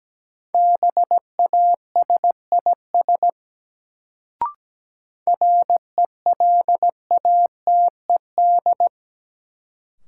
Morse code, heard as a keyed tone, about 17 wpm, 700 Hz. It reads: BASIS RELATED
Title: Sentences from Top 1000 Words - Review Part 5 - New Word Drill - 17wpm